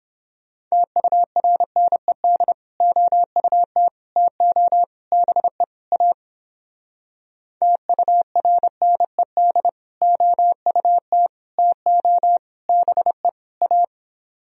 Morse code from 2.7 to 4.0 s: OUT

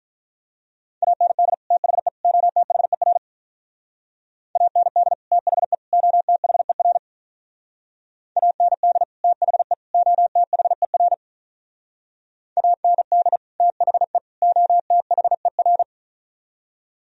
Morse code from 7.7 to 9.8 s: AND THE

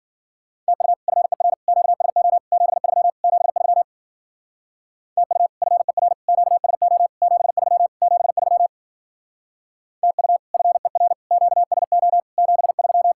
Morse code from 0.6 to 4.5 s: TU FER QSO 73 73

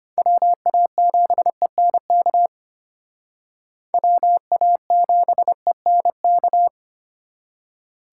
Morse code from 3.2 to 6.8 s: WA7ENK

